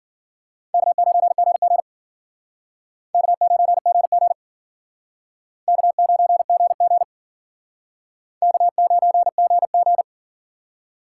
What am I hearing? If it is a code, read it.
Morse code, heard as a keyed tone, about 53 wpm, 700 Hz. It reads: K9GG K9GG K9GG K9GG